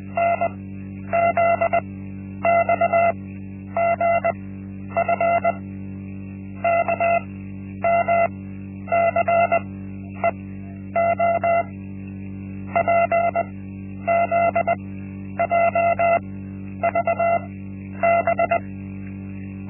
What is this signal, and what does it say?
Morse code, heard as a keyed tone, about 20 wpm, spaced as for 10 wpm, 680 Hz. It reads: NZXGF KMCEO PZJVB
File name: Linksmakalnis-1